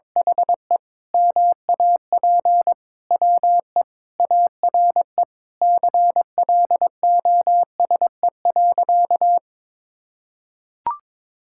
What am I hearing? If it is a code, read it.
Morse code, heard as a keyed tone, about 22 wpm, 700 Hz. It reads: HE MAP WE ARE CLOSE.
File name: practice / 22wpm / Sentences with 2-Word Phrases - 22wpm